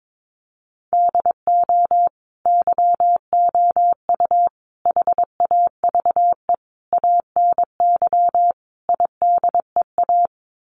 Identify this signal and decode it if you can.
Morse code, heard as a keyed tone, about 22 wpm, 700 Hz. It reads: DO YOU HAVE ANY IDEA
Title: Sets of 5 Words Encore 22wpm